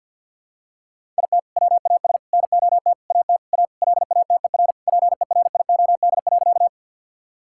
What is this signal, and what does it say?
Morse code, heard as a keyed tone, about 50 wpm, 700 Hz. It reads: IT WAS NOT AT A LATER PERIOD.
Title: Sentences from Top 1000 Words - Review Part 5 - 50wpm